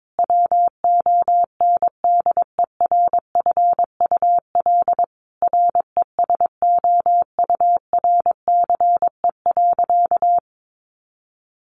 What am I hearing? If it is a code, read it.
Morse code, heard as a keyed tone, about 22 wpm, 700 Hz. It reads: WONDERFUL RESOURCE.
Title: Sentences from Top 1300 Words 22wpm